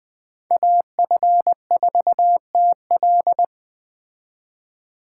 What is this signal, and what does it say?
Morse code, heard as a keyed tone, about 20 wpm, 700 Hz. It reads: AF4TL
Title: Speed Racing - USA 2x2 Call Signs 17wpm